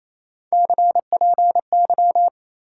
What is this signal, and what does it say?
Morse code, heard as a keyed tone, about 35 wpm, 700 Hz. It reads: CPY
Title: Speed-Racing Top 90 QSO Elements 28wpm